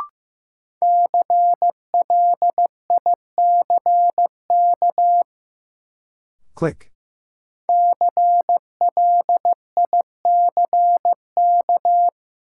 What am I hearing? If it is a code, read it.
Morse code, heard as a keyed tone, about 15 wpm, 700 Hz. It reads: CLICK CLICK
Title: Sentences from Top 1100 Words - New Word Drill Alt - 15wpm